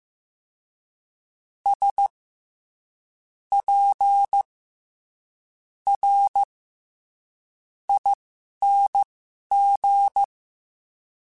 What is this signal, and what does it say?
Morse code, heard as a keyed tone, about 15 wpm, spaced as for 11 wpm, 800 Hz. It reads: S P R ING
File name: Special INGs-mp3s / Special INGs-mp3s / SlowToFaster ing @10